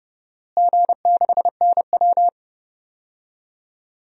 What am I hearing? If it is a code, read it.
Morse code, heard as a keyed tone, about 30 wpm, 700 Hz. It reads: G6NW
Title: DXCC-100 Call Signs - 4 Characters - Repeated 2x - 30wpm